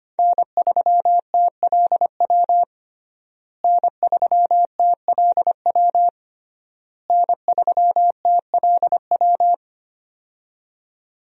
Morse code, heard as a keyed tone, about 25 wpm, 700 Hz. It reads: N3TLW N3TLW N3TLW
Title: USA Call Signs - 5 Characters - Repeated 3x - 25wpm